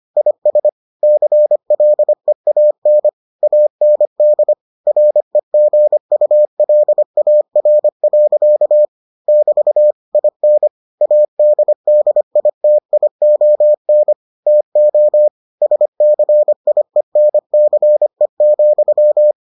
Morse code, heard as a keyed tone, about 25 wpm, 600 Hz. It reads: IS CLEAN AND REGULAR. <BT> IN ADDITION TO SCIENCE,